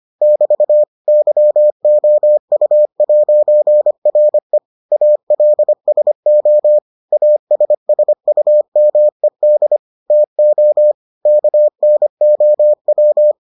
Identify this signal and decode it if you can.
Morse code, heard as a keyed tone, about 25 wpm, 600 Hz. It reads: <BT> YOU'RE ALSO ASSUMED TO KNOW